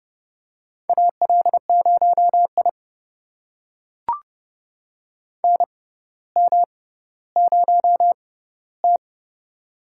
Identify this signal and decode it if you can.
Morse code, heard as a keyed tone, about 29 wpm, spaced as for 15 wpm, 700 Hz. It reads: AL0I NM0T
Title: USA Call Signs - 4 Characters - 6x Character Spacing - 30wpm